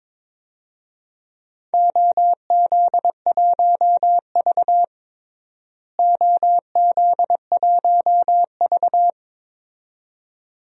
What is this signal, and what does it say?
Morse code, heard as a keyed tone, about 21 wpm, 700 Hz. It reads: OZ1V OZ1V
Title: DXCC-100 Call Signs - 4 Characters - Repeated 2x - 22wpm